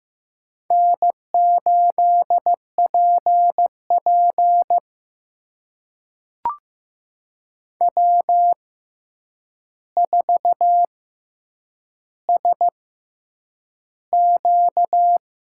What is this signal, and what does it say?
Morse code, heard as a keyed tone, about 15 wpm, spaced as for 8 wpm, 700 Hz. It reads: N8PP W4SQ
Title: USA Call Signs - 4 Characters - 6x Character Spacing - 15wpm